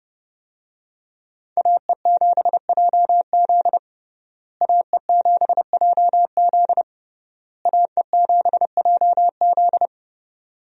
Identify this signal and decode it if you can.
Morse code, heard as a keyed tone, about 30 wpm, 700 Hz. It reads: AE7JZ AE7JZ AE7JZ
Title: USA Call Signs - 5 Characters - Repeated 3x - 30wpm